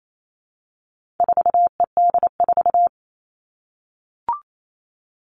Morse code, heard as a keyed tone, about 28 wpm, 700 Hz. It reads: 4ED4
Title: All Cut Numbers - 4-Digit 28wpm